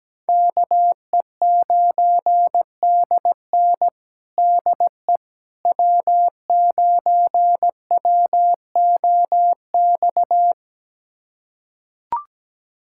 Morse code, heard as a keyed tone, about 17 wpm, 700 Hz. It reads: KE9DN DE W9WOX
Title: USA Callsign DE Callsign - 5 Characters - 17wpm